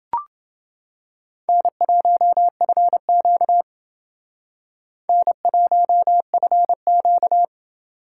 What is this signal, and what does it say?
Morse code, heard as a keyed tone, about 30 wpm, 700 Hz. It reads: N1FQ N1FQ